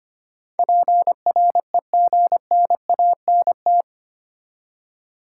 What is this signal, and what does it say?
Morse code, heard as a keyed tone, about 25 wpm, 700 Hz. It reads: PREGNANT
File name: Eight Letter Words - 25wpm